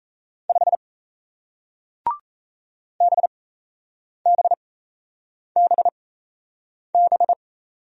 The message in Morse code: 5 BBBB

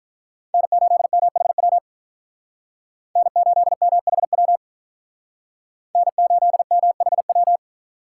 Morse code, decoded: N8MHW N8MHW N8MHW

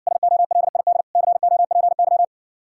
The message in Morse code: SOLID COPY

